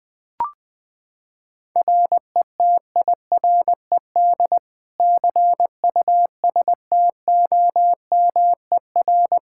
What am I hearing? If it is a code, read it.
Morse code, heard as a keyed tone, about 20 wpm, 700 Hz. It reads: RETIRED CUSTOMER
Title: Practice QSO - Exchange 3 - More Info - 20wpm